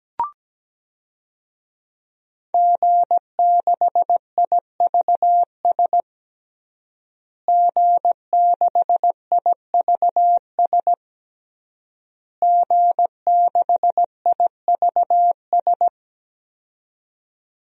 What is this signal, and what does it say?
Morse code, heard as a keyed tone, about 17 wpm, 700 Hz. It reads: G6IVS G6IVS G6IVS
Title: DXCC-100 Call Signs - 5 Characters - Repeated 3x - 17wpm